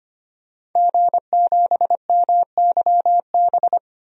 Morse code, heard as a keyed tone, about 25 wpm, 700 Hz. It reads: G7MYB
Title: DXCC-100 Call Signs - 5 Characters - 25wpm